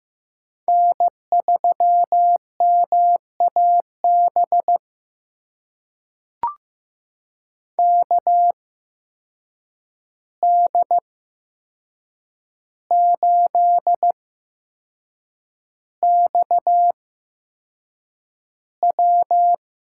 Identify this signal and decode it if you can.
Morse code, heard as a keyed tone, about 15 wpm, spaced as for 6 wpm, 700 Hz. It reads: N3MAB KD8XW
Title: USA Call Signs - 5 Characters - 8x Character Spacing - 15wpm